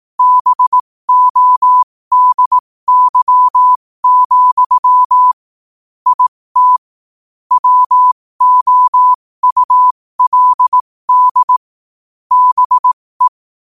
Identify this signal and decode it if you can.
Morse code, heard as a keyed tone, about 18 wpm, spaced as for 15 wpm, 1000 Hz. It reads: BODY, IT WOULD BE